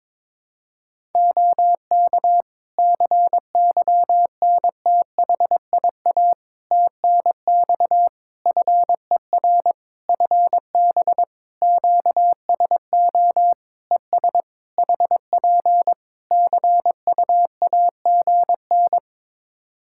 Morse code, heard as a keyed tone, about 22 wpm, 700 Hz. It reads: OK CYNTHIA TNX FER FB QSO ES HP CUAGN
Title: K1USN SST QSO Practice 22wpm